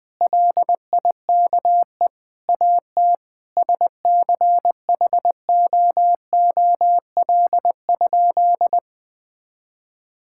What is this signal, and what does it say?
Morse code, heard as a keyed tone, about 20 wpm, 700 Hz. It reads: LIKE AT SCHOOL?
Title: Sentences from Top 1000 Words - Review Part 6 - 20wpm